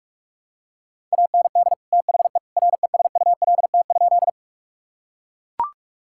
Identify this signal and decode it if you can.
Morse code, heard as a keyed tone, about 42 wpm, 700 Hz. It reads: AND THE RESULT?
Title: Sentences from Top 1000 Words - Review Part 5 - 45wpm